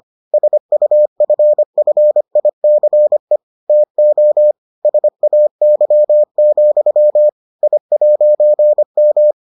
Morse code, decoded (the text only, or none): SUFFICE TO SAY, I'M